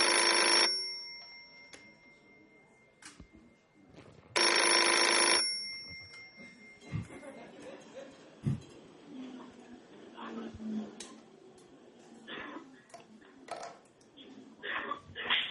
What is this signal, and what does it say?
Dial-up modem handshake, with speech.